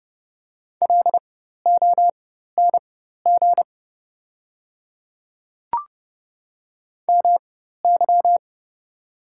Morse code, heard as a keyed tone, about 29 wpm, spaced as for 14 wpm, 700 Hz. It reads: LONG MY
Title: 4x Character Spacing - Top 200 Words - 30wpm